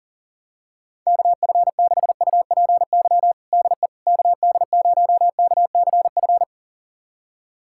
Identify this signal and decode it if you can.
Morse code, heard as a keyed tone, about 40 wpm, 700 Hz. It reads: KF6UPY DE KD0KCF
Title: USA Callsign DE Callsign - 6 Characters - 40wpm